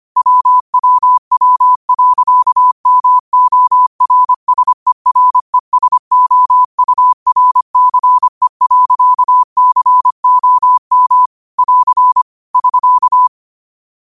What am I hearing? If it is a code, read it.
Morse code, heard as a keyed tone, about 25 wpm, 1000 Hz. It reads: WWW.MORSERESOURCE.COM <AR> <SK>
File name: qotdfiles / 25wpm